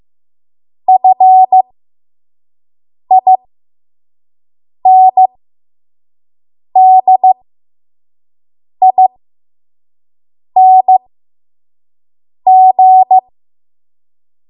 Morse code, heard as a keyed tone, about 15 wpm, spaced as for 5 wpm, 750 Hz. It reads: FINDING
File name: Morse / 05WPM